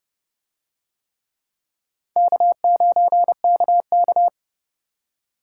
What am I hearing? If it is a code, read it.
Morse code, heard as a keyed tone, about 29 wpm, 700 Hz. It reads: K9KK